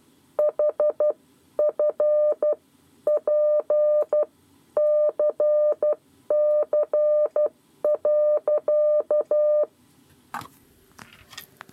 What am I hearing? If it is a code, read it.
Morse code, heard as a keyed tone, about 12 wpm, spaced as for 10 wpm, 600 Hz. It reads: HFPCC.
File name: 7 Lezione cw 2